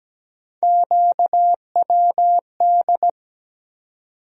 Morse code, heard as a keyed tone, about 17 wpm, 700 Hz. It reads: QWD
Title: USA Call Sign Postfixes - 3 Letters 17wpm